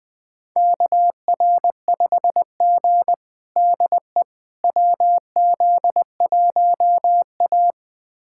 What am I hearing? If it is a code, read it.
Morse code, heard as a keyed tone, about 20 wpm, 700 Hz. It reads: KR5G DE WZ1A
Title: USA Callsign DE Callsign - 4 Characters - 20wpm